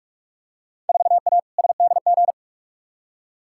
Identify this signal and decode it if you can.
Morse code, heard as a keyed tone, about 45 wpm, 700 Hz. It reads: 4A SDG
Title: Field Day Exchanges Encore - 45wpm